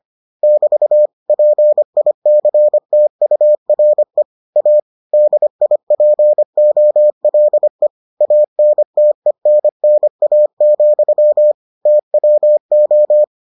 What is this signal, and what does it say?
Morse code, heard as a keyed tone, about 25 wpm, 600 Hz. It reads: <BT> PICTURE A DIPOLE ANTENNA, TWO